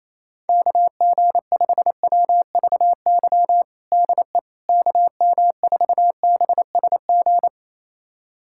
Morse code, heard as a keyed tone, about 28 wpm, 700 Hz. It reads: KG5WVY DE KM4BSG
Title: USA Callsign DE Callsign - 6 Characters - 28wpm